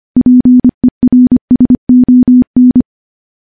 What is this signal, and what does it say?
Morse code, heard as a keyed tone, about 25 wpm, 250 Hz. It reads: PERSON
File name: morse-training-25WPM